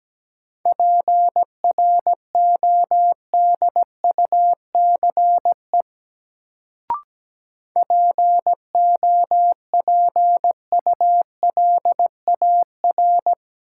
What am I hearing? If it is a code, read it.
Morse code, heard as a keyed tone, about 17 wpm, 700 Hz. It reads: PRODUCE POPULAR
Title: Sentences from Top 1000 Words - Review Part 3 - New Word Drill - 17wpm